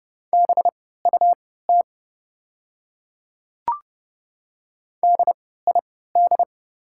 Morse code, heard as a keyed tone, about 30 wpm, spaced as for 17 wpm, 700 Hz. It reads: BUT DID
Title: 3x Character Spacing - Top 100 Words - 30wpm